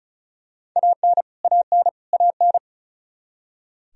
Morse code, heard as a keyed tone, about 34 wpm, 700 Hz. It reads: AN AN AN
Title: Top 100 Words Repeated 35wpm